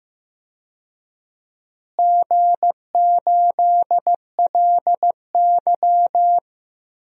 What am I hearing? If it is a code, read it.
Morse code, heard as a keyed tone, about 15 wpm, 700 Hz. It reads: G8LY
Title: DXCC-100 Call Signs - 4 Characters - Repeated 2x - 15wpm